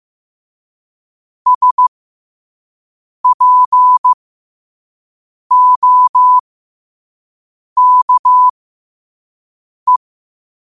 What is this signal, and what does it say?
Morse code, heard as a keyed tone, about 15 wpm, spaced as for 5 wpm, 1000 Hz. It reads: SPOKE
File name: qotdfiles / 5wpm